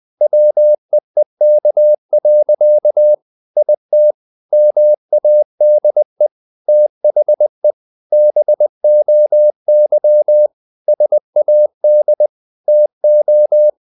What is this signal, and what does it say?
Morse code, heard as a keyed tone, about 20 wpm, 600 Hz. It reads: WEEK. IT MADE THE BOY SAD TO
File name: the old man and the sea / the old man and the sea part 1 20wpm